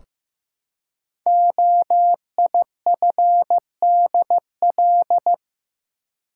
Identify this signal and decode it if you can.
Morse code, heard as a keyed tone, about 15 wpm, 700 Hz. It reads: OIFDL
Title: Code Groups - 5 Letters - 15wpm